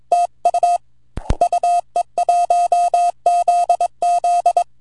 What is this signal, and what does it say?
Morse code, heard as a keyed tone, about 24 wpm, 700 Hz. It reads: TU UE1ZZ
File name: vq9la / VE1ZZ 10AUG08 0104Z 160mtr